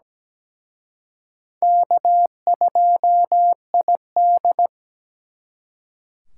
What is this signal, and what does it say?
Morse code, heard as a keyed tone, about 20 wpm, 700 Hz. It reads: K2ID
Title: Speed Racing - USA 1x2 Call Signs 17wpm